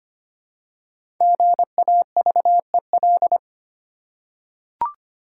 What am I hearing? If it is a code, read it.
Morse code, heard as a keyed tone, about 25 wpm, 690 Hz. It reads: GAVEL
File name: practice / 25wpm / Easy Words with Hints - 25wpm